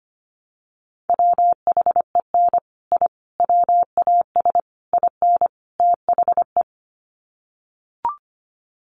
Morse code, heard as a keyed tone, about 25 wpm, 700 Hz. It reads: WHEN I WAS IN THE
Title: Sets of 5 Words Encore2 25wpm